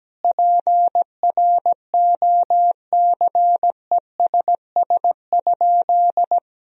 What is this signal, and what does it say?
Morse code, heard as a keyed tone, about 17 wpm, 700 Hz. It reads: PROCESS?
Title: Sentences from Top 1000 Words - Review Part 3 - 17wpm